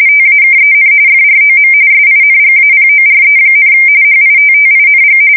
RTTY, about 45.45 baud. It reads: TAKEN FROM SINGAPORE, HITTING
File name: fsk170 / USS-New-Jersey